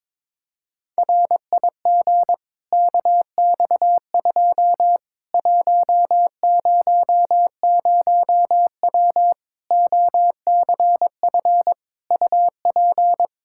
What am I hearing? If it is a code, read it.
Morse code, heard as a keyed tone, about 22 wpm, 700 Hz. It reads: RIG KX2 100W OCF UP
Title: Practice QSO - Exchange 2 - Info - 22wpm